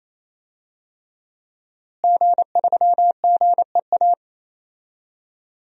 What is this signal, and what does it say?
Morse code, heard as a keyed tone, about 28 wpm, 700 Hz. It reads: G3GEA
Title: DXCC-100 Call Signs - 5 Characters - Repeated 3x - 28wpm